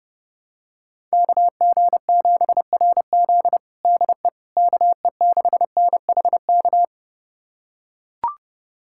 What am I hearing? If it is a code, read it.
Morse code, heard as a keyed tone, about 30 wpm, 700 Hz. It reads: KG7RZ DE KE6NHK
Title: USA Callsign DE Callsign 30wpm